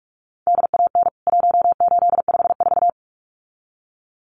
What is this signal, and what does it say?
Morse code, heard as a keyed tone, about 44 wpm, 700 Hz. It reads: DAN 1854
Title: CWT Exchange 45wpm